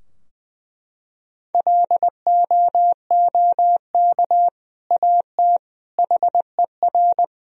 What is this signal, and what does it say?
Morse code, heard as a keyed tone, about 20 wpm, 700 Hz. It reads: LOOK AT HER
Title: Sentences from Top 1000 Words - Review Part 2 - 20wpm